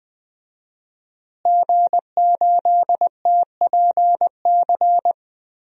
Morse code, heard as a keyed tone, about 20 wpm, 700 Hz. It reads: G8TPC
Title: DXCC-100 Call Signs - 5 Characters - Repeated 3x - 20wpm